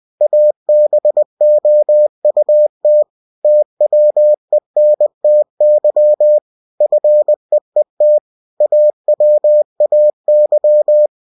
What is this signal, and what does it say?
Morse code, heard as a keyed tone, about 20 wpm, 600 Hz. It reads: ABOUT TWENTY FEET AWAY
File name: cw-books / winnie-the-pooh / winnie-20wpm-page-0006